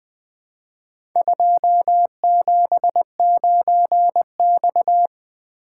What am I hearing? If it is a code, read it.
Morse code, heard as a keyed tone, about 20 wpm, 700 Hz. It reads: 279X
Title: Four Letter-Number Combinations - 20wpm